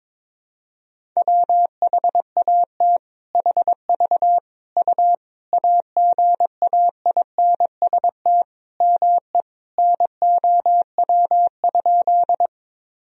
Morse code, heard as a keyed tone, about 22 wpm, 700 Hz. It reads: WHAT HV U AGAINST ME NOW?
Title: Sentences from Top 400 Words and Abbreviations 22wpm